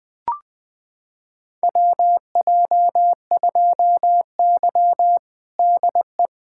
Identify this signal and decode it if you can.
Morse code, heard as a keyed tone, about 20 wpm, 700 Hz. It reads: WJ2Y DE
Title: USA Callsign DE Callsign - 4 Characters - 20wpm